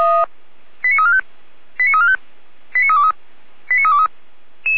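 Five-tone selective call (ZVEI), heard as quick runs of five tones, tones of about 80 ms. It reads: 89345 89345 89323 89323
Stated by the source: ZVEI five-tone call 89230 received by SDR, from the feuerwehr-orlamuende.org server (fire brigade Orlamünde).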